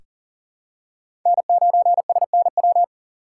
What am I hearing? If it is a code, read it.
Morse code, heard as a keyed tone, about 40 wpm, 700 Hz. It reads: N9SNW
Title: USA Call Signs - 5 Characters - 2x Character Spacing - 40wpm